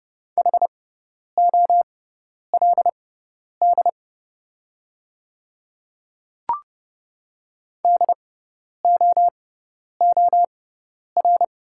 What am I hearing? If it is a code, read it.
Morse code, heard as a keyed tone, about 30 wpm, spaced as for 10 wpm, 700 Hz. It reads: HOLD DOOR
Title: 6x Character Spacing - Top 500 Words - 30wpm